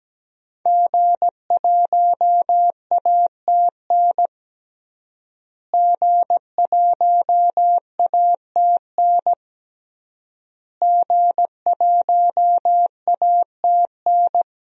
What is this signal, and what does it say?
Morse code, heard as a keyed tone, about 17 wpm, 700 Hz. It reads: G1ATN G1ATN G1ATN
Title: DXCC-100 Call Signs - 5 Characters - Repeated 3x - 17wpm